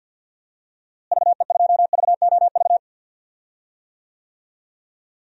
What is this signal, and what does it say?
Morse code, heard as a keyed tone, about 52 wpm, 700 Hz. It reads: VE2VOV
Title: DXCC-100 Call Signs - 6 Characters - Repeated 3x - 50wpm